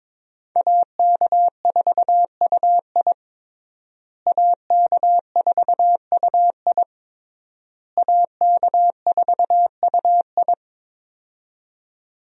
Morse code, heard as a keyed tone, about 22 wpm, 700 Hz. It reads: AK4UI AK4UI AK4UI
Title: USA Call Signs - 5 Characters - Repeated 3x - 22wpm